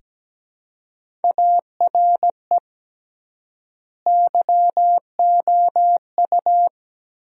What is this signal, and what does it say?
Morse code, heard as a keyed tone, about 17 wpm, 700 Hz. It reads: ARE YOU
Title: 3x Word-Spacing - Sentences from Top 500 Words - 17wpm